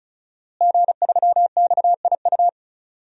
Morse code, heard as a keyed tone, about 36 wpm, 700 Hz. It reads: G3XIU
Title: DXCC-100 Call Signs - 5 Characters - Repeated 2x - 35wpm